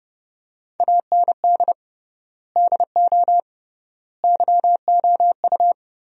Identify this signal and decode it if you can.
Morse code, heard as a keyed tone, about 30 wpm, 700 Hz. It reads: AND DO YOU
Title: 3x Word-Spacing - Sentences from Top 100 Words - 30wpm